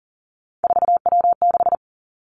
Morse code, heard as a keyed tone, about 41 wpm, 700 Hz. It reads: 4W6
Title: Three Challenging Letter-Number Combinations 40wpm